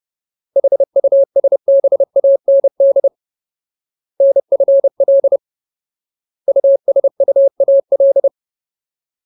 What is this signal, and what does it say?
Morse code, heard as a keyed tone, about 30 wpm, 550 Hz. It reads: HUSBAND NFL USUAL
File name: Ultimate Word List / 30WPM Ultimate